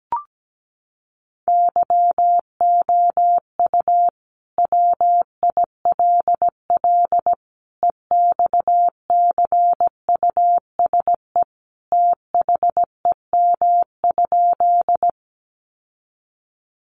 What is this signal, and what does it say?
Morse code, heard as a keyed tone, about 17 wpm, 700 Hz. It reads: YOU WILL EXCUSE THEM?
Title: Sentences from Top 700 Words 17wpm